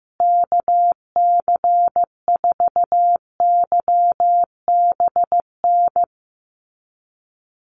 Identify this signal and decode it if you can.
Morse code, heard as a keyed tone, about 15 wpm, 700 Hz. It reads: KC4YBN
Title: USA Call Signs Repeated 3x 15wpm